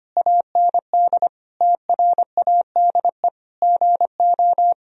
Morse code, heard as a keyed tone, about 25 wpm, 700 Hz. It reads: AND TRADE GO
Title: Sentences from Top 1000 Words - Review Part 4 - 25wpm